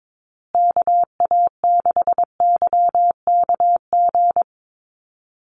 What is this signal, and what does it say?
Morse code, heard as a keyed tone, about 22 wpm, 700 Hz. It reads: KA6YKG
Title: USA Call Signs Repeated 2x 22wpm